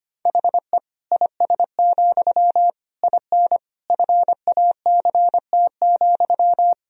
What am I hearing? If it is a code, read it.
Morse code, heard as a keyed tone, about 25 wpm, 700 Hz. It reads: HE IS, IN FACT,
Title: Sentences from Top 1000 Words - Review Part 3 - 25wpm